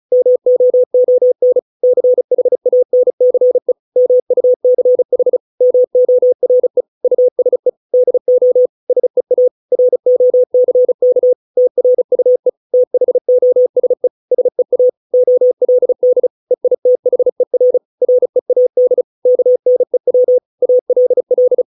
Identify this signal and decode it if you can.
Morse code, heard as a keyed tone, about 34 wpm, 500 Hz. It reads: MOON CHANCE MUCH MORE USE DO SEA ROCK TRUE THOSE SEA OLD EITHER READ KNEW ALL